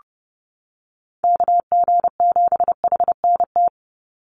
Morse code, heard as a keyed tone, about 30 wpm, 700 Hz. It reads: KG7HNT
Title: USA Call Signs 30wpm